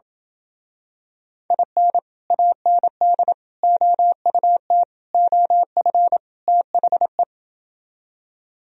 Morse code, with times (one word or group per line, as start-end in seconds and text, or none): 1.5-2.0 IN
2.3-3.3 AND
3.6-4.8 OUT
5.1-6.2 OF
6.5-7.2 THE